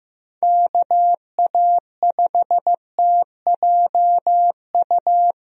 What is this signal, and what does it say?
Morse code, heard as a keyed tone, about 16 wpm, 700 Hz. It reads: KA5TJU
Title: USA Call Signs - 6 Characters - Repeated 2x - 15wpm